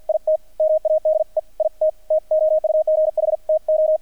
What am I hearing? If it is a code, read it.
Morse code, heard as a keyed tone, about 51 wpm, spaced as for 45 wpm, 630 Hz, with radio noise. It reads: IT MADE IT TOUGH TO
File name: archived / CRSnet